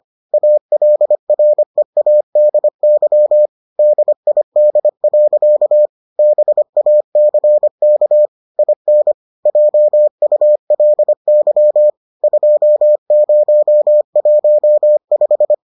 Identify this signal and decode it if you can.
Morse code, heard as a keyed tone, about 25 wpm, 600 Hz. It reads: ALREADY DID. BACK IN JULY 2015